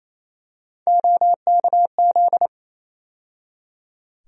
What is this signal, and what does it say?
Morse code, heard as a keyed tone, about 28 wpm, 700 Hz. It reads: OKZ
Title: USA Call Sign Postfixes - 3 Letters 28wpm